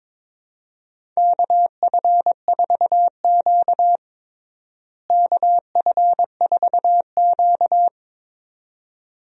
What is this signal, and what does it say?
Morse code, heard as a keyed tone, about 22 wpm, 700 Hz. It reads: KF4Q KF4Q
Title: USA Call Signs - 4 Characters - Repeated 2x - 22wpm